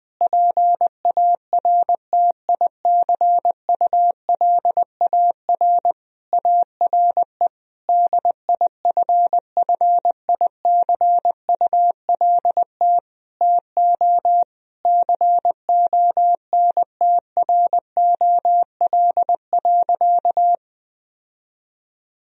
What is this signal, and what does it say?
Morse code, heard as a keyed tone, about 21 wpm, 700 Hz. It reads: PARTICULAR ARE DIFFICULT TO CONTROL.
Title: Sentences from Top 1000 Words - Review Part 4 - 20wpm